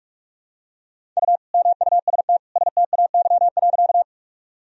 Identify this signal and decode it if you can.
Morse code, heard as a keyed tone, about 46 wpm, 700 Hz. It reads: U MUST STAY.
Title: Sentences from Top 500 Words and Abbreviations 45wpm